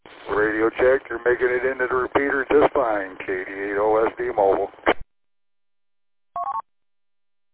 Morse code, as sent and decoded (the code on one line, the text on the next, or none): none